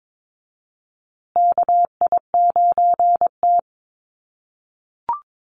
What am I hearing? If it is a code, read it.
Morse code, heard as a keyed tone, about 22 wpm, 700 Hz. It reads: KI9T
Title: USA Call Signs Repeated 3x 22wpm